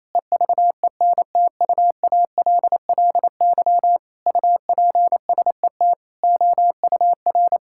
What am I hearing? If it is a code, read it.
Morse code, heard as a keyed tone, about 28 wpm, 700 Hz. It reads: EVENTUALLY UPSET OUR
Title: Sentences from Top 1000 Words - Review Part 6 - 28wpm